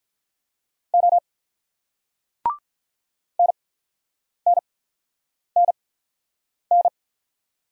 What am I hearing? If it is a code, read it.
Morse code, heard as a keyed tone, about 48 wpm, spaced as for 9 wpm, 700 Hz. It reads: O NNNN